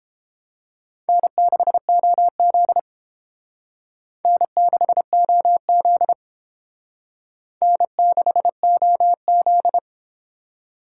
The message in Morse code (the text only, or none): N6OZ N6OZ N6OZ